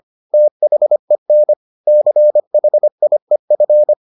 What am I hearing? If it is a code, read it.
Morse code, heard as a keyed tone, about 25 wpm, 600 Hz. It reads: THEN CHIEF